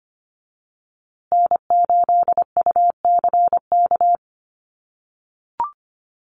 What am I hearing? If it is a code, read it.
Morse code, heard as a keyed tone, about 25 wpm, 700 Hz. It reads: N8UCK
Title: USA Call Signs Repeated 2x 25wpm